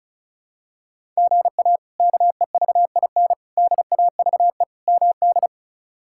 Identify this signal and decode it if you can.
Morse code, heard as a keyed tone, about 36 wpm, 700 Hz. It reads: GA KEVIN DAVE MD
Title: K1USN SST QSO Practice - 35wpm